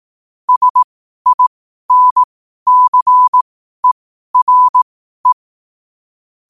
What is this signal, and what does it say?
Morse code, heard as a keyed tone, about 18 wpm, spaced as for 12 wpm, 1000 Hz. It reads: SINCERE
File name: qotdfiles / 12wpm